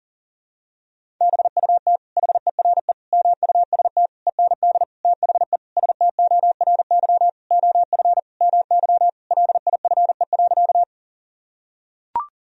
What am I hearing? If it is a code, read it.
Morse code, heard as a keyed tone, about 40 wpm, 700 Hz. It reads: BUT HERE MUST END THE STORY OF MY LIFE.
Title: Sentences from Top 1000 Words - Review Part 1 - 40wpm